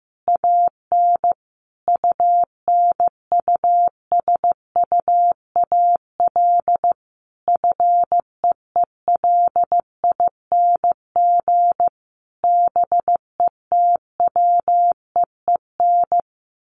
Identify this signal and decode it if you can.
Morse code, heard as a keyed tone, about 14 wpm, 700 Hz. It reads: AN UNUSUAL FEELING BETWEEN
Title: Sentences from Top 1400 Words 15wpm